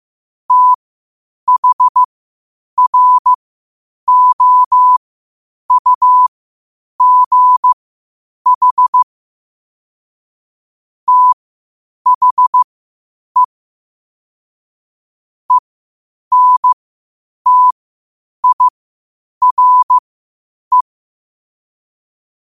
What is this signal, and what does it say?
Morse code, heard as a keyed tone, about 15 wpm, spaced as for 8 wpm, 1000 Hz. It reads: THROUGH THE ENTIRE
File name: qotdfiles / 8wpm